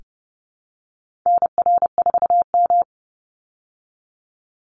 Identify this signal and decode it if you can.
Morse code, heard as a keyed tone, about 30 wpm, 700 Hz. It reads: NR4M
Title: Call Signs Repeated 2x - 30wpm